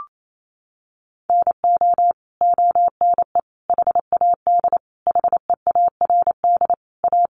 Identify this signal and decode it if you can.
Morse code, heard as a keyed tone, about 28 wpm, 700 Hz. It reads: NO ONE HAD HEARD A
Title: Sentences from Top 1100 Words 28wpm